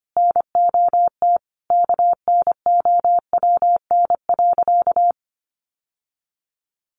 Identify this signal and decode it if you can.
Morse code, heard as a keyed tone, about 25 wpm, 700 Hz. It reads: NOT KNOWN.
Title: Sentences from Top 1100 Words 25wpm